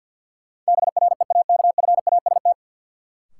Morse code, heard as a keyed tone, about 53 wpm, 700 Hz. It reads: BREAKFAST
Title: Sentences from Top 1500 Words - New Word Drill - 50wpm